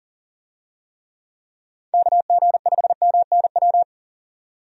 Morse code, heard as a keyed tone, about 39 wpm, 700 Hz. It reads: KG5MNW
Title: USA Call Signs - 6 Characters - 40wpm